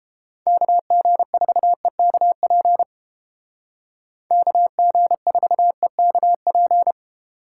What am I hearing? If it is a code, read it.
Morse code, heard as a keyed tone, about 31 wpm, 700 Hz. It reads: KG4EKP KG4EKP